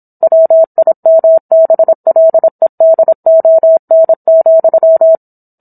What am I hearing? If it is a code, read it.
Morse code, heard as a keyed tone, about 27 wpm, 650 Hz. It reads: WIMBLEDON,